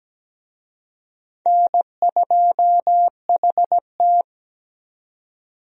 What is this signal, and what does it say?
Morse code, heard as a keyed tone, about 17 wpm, 700 Hz. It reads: N2HT
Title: USA Call Signs - 4 Characters - 17wpm